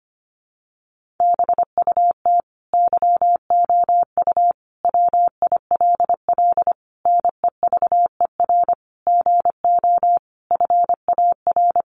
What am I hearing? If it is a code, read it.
Morse code, heard as a keyed tone, about 25 wpm, 700 Hz. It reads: BUT YOU WILL NEVER GO FAR
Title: Sentences from Top 1500 Words 25wpm